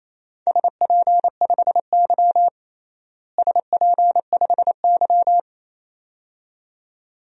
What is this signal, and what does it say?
Morse code, heard as a keyed tone, about 28 wpm, 700 Hz. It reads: SP5Y SP5Y
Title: DXCC-100 Call Signs - 4 Characters - Repeated 2x - 28wpm